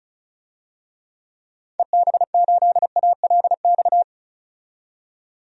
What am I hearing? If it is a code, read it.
Morse code, heard as a keyed tone, about 36 wpm, 700 Hz. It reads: EB8ALX
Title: DXCC-100 Call Signs - 6 Characters - 35wpm